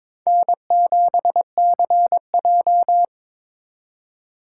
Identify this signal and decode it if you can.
Morse code, heard as a keyed tone, about 26 wpm, 700 Hz. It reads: N7CJ
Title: Speed Racing - USA 1x2 Call Signs 22wpm